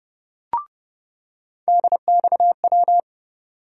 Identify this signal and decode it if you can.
Morse code, heard as a keyed tone, about 30 wpm, 700 Hz. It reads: DXW
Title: USA Call Sign Postfixes - 3 Letters 30wpm